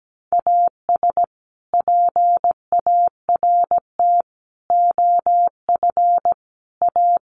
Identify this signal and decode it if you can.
Morse code, heard as a keyed tone, about 17 wpm, 700 Hz. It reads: AS PART OF A
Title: Sets of 4 Words 17wpm